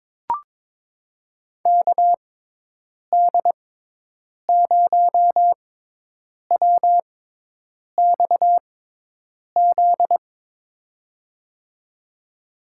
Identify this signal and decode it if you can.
Morse code, heard as a keyed tone, about 22 wpm, spaced as for 8 wpm, 700 Hz. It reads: KD0WXZ
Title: USA Call Signs - 6 Characters - 6x Character Spacing - 22wpm